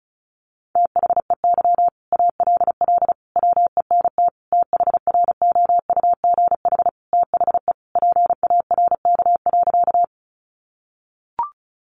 Morse code, heard as a keyed tone, about 34 wpm, 700 Hz. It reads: THEY ALL WENT THROUGH THE PARK.
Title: Sentences from Top 1300 Words 35wpm